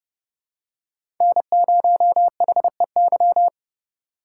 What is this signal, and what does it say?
Morse code, heard as a keyed tone, about 30 wpm, 700 Hz. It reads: N0HEY
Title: USA Call Signs - 5 Characters - 4x Character Spacing - 30wpm